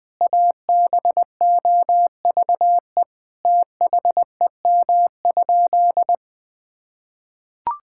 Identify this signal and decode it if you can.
Morse code, heard as a keyed tone, about 20 wpm, 700 Hz. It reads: ABOVE THEM?
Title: Sentences from Top 1000 Words - Review Part 5 - 20wpm